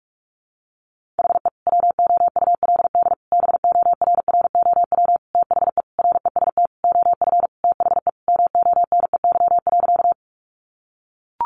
Morse code, heard as a keyed tone, about 44 wpm, 700 Hz. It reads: HE WOULD BORROW THE REST OF THE MONEY.
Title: Sentences from Top 1500 Words 45wpm